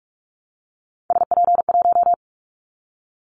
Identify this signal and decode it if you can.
Morse code, heard as a keyed tone, about 44 wpm, 700 Hz. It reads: SP1
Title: Callsign Prefixes for Top 25 DXCC - Repeated 3x - 45wpm